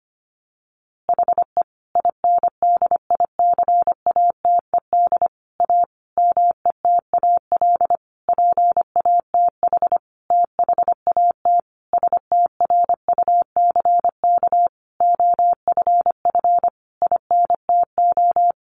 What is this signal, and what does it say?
Morse code, heard as a keyed tone, about 25 wpm, 700 Hz. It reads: HE INDICATED A METAL PATH THAT STRUCK OFF INTO
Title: A Sound Of Thunder 25wpm